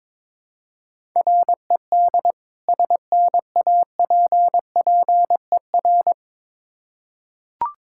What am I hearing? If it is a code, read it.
Morse code, heard as a keyed tone, about 22 wpm, 700 Hz. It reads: RED SNAPPER
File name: Difficult Words with Hints - 22wpm